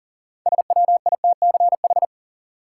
Morse code, heard as a keyed tone, about 37 wpm, 700 Hz. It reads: SWITCH
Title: Six Letter Words - 40wpm